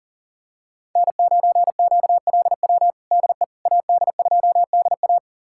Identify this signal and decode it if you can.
Morse code, heard as a keyed tone, about 40 wpm, 700 Hz. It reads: N9QLW DE AD2DA
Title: USA Callsign DE Callsign - 5 Characters - 40wpm